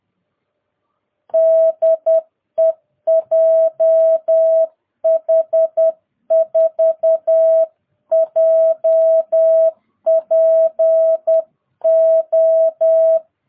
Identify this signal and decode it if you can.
Morse code, heard as a keyed tone, about 10 wpm, 650 Hz. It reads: DE JH4JPO
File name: jh8cbh / 2022cw